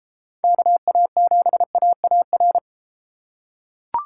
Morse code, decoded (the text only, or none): KA7AAR